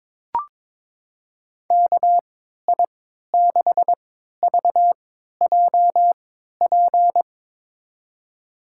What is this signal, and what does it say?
Morse code, heard as a keyed tone, about 21 wpm, spaced as for 12 wpm, 700 Hz. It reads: KI6VJP
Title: USA Call Signs - 6 Characters - 3x Character Spacing - 22wpm